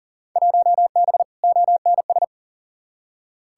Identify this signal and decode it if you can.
Morse code, heard as a keyed tone, about 40 wpm, 700 Hz. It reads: 1B ONS
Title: Field Day Exchanges Encore - 40wpm